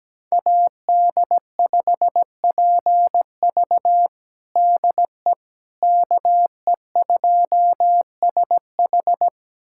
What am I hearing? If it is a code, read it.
Morse code, heard as a keyed tone, about 17 wpm, 700 Hz. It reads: AD5PV DE KE2SH
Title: USA Callsign DE Callsign - 5 Characters - 17wpm